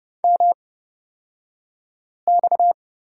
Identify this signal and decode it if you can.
Morse code, heard as a keyed tone, about 30 wpm, spaced as for 5 wpm, 700 Hz. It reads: MX
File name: Lesson 37d - Letter x - Two Characters 30wpm